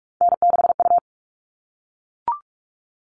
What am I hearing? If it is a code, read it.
Morse code, heard as a keyed tone, about 44 wpm, 700 Hz. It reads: N6U
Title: All Cut Numbers - 3-Digit 45wpm